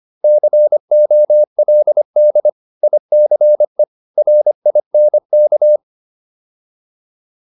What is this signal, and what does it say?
Morse code, heard as a keyed tone, about 25 wpm, 600 Hz. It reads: COLD ICE RINK